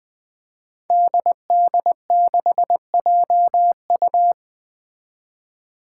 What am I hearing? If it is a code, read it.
Morse code, heard as a keyed tone, about 20 wpm, 700 Hz. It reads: DD6JU
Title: Five Challenging Letter-Number Combinations 20wpm